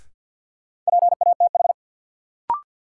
Morse code, heard as a keyed tone, about 52 wpm, 700 Hz. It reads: PATH
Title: Sentences from Top 1400 Words - New Word Drill - 50wpm